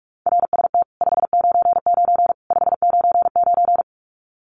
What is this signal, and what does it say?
Morse code, heard as a keyed tone, about 44 wpm, 700 Hz. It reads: RST 599 599
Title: Common QSO Phrases 45wpm